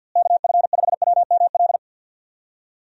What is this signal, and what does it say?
Morse code, heard as a keyed tone, about 49 wpm, 700 Hz. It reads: KF5WML
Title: USA Call Signs - 6 Characters - Repeated 3x - 50wpm